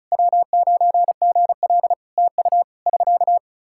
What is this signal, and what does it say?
Morse code, heard as a keyed tone, about 36 wpm, 700 Hz. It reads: W9GL TU <SK>
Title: K1USN SST QSO Practice 35wpm